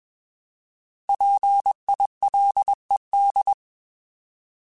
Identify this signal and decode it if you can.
Morse code, heard as a keyed tone, about 21 wpm, 800 Hz. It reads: PILED
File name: Alphabetical / Alphabetical mp3s / Ps @20